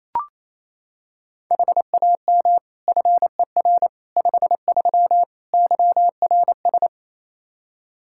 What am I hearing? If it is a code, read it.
Morse code, heard as a keyed tone, about 28 wpm, 700 Hz. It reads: HAM FER 53 YRS
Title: Practice QSO - Exchange 3 - More Info - 28wpm